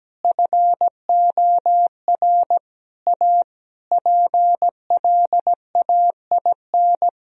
Morse code, heard as a keyed tone, about 17 wpm, 700 Hz. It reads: FOR A PLAIN
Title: Sentences from Top 1000 Words - Review Part 2 - 17wpm